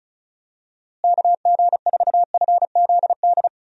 Morse code, heard as a keyed tone, about 36 wpm, 700 Hz. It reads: KG4FZD
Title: USA Call Signs - 6 Characters - Repeated 3x - 35wpm